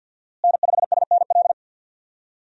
Morse code, heard as a keyed tone, about 53 wpm, 700 Hz. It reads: N5SNL
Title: USA Call Signs - 5 Characters - Repeated 2x - 50wpm